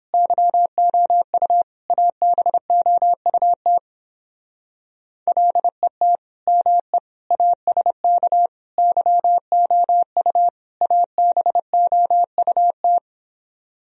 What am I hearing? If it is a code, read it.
Morse code, heard as a keyed tone, about 31 wpm, 700 Hz. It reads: YOU ABOUT LET ME ASK YOU ABOUT